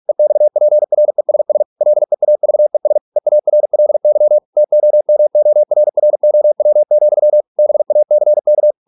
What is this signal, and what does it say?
Morse code, heard as a keyed tone, about 46 wpm, 600 Hz. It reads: EXPRESS LEAVES EARLY TOMORROW, BACK